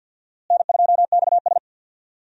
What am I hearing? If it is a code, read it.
Morse code, heard as a keyed tone, about 53 wpm, 700 Hz. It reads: N2XS
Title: USA Call Signs - 4 Characters - Repeated 2x - 50wpm